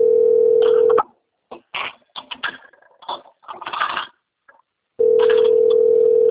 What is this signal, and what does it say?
Ringback tone (North America), 440 + 480 Hz.
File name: rg-1600-unknown- - - .26358